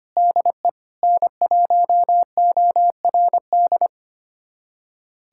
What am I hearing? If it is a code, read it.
Morse code, heard as a keyed tone, about 25 wpm, 700 Hz. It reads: DE N1ORD
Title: USA Callsign DE Callsign - 5 Characters - 25wpm